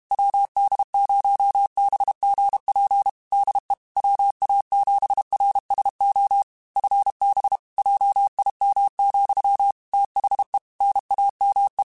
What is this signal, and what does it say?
Morse code, heard as a keyed tone, about 32 wpm, 800 Hz. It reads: WD0BGP DE WA7RSO FB JIM, THE NAME